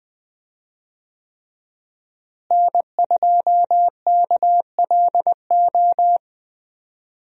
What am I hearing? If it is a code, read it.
Morse code, heard as a keyed tone, about 21 wpm, 700 Hz. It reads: N2KLO
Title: USA Call Signs - 5 Characters - Repeated 3x - 20wpm